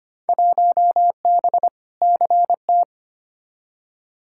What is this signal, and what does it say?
Morse code, heard as a keyed tone, about 25 wpm, 700 Hz. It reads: 1B CT
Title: Field Day Exchanges Encore - 25wpm